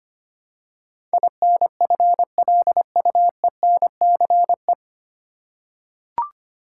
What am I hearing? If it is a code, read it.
Morse code, heard as a keyed tone, about 25 wpm, 700 Hz. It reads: INFLUENCE
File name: practice / 25wpm / 9 Letter Words - First 3 Letters Given - 25wpm